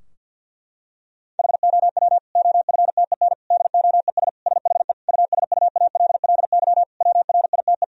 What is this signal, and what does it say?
Morse code, heard as a keyed tone, about 51 wpm, 700 Hz. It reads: HOW OFTEN DOES SHE USUALLY WRITE
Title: Sentences from Top 1000 Words - Review Part 3 - 50wpm